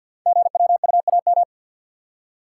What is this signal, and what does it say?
Morse code, heard as a keyed tone, about 46 wpm, 700 Hz. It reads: GWUAM
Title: Code Groups - 5 Letters - 50wpm